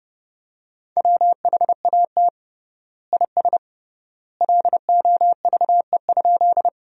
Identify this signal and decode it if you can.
Morse code, heard as a keyed tone, about 30 wpm, 700 Hz. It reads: WHAT IS LOVE?
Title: 3x Word-Spacing - Sentences from Top 400 Words - 30wpm